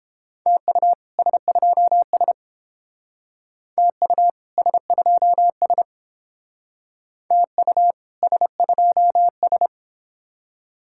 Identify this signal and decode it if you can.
Morse code, heard as a keyed tone, about 30 wpm, 700 Hz. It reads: TU S2S TU S2S TU S2S